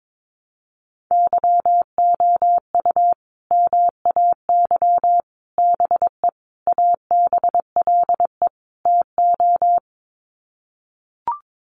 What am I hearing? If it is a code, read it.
Morse code, heard as a keyed tone, about 22 wpm, 700 Hz. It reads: YOU MAY BE ABLE TO
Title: Sets of 5 Words Encore 22wpm